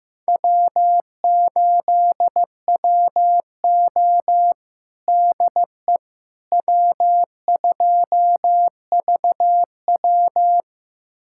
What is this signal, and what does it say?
Morse code, heard as a keyed tone, about 15 wpm, 700 Hz. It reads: W8WO DE W2VW
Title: USA Callsign DE Callsign - 4 Characters - 15wpm